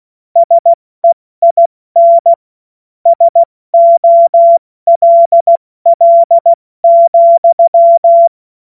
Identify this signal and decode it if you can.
Morse code, heard as a keyed tone, about 16 wpm, 680 Hz. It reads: SEIN SOLL,